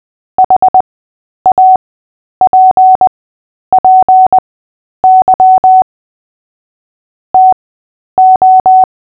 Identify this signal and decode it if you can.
Morse code, heard as a keyed tone, about 20 wpm, spaced as for 10 wpm, 750 Hz. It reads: HAPPY TO
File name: cw / fists / Short-Story-8-10wpm